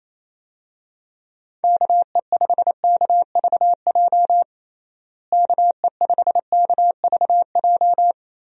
Morse code, heard as a keyed tone, about 28 wpm, 700 Hz. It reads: KE5KVJ KE5KVJ
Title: USA Call Signs - 6 Characters - Repeated 3x - 28wpm